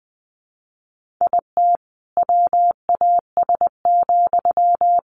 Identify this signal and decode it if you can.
Morse code, heard as a keyed tone, about 20 wpm, 700 Hz. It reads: IT WAS,
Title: Sentences from Top 1500 Words 20wpm